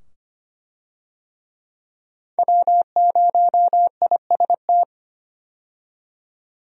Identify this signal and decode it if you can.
Morse code, heard as a keyed tone, about 25 wpm, 700 Hz. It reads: W0IST
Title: USA Call Signs - 5 Characters - 25wpm